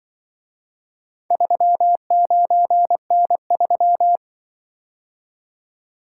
Morse code, heard as a keyed tone, about 23 wpm, 700 Hz. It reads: 39N3
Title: Speed-Racing - Four Letter-Number Combinations - 20wpm